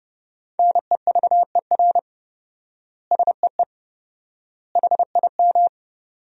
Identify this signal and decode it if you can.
Morse code, heard as a keyed tone, about 30 wpm, 700 Hz. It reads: NEVER SEE HIM
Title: 4x Word-Spacing - Sentences from Top 300 Words - 30wpm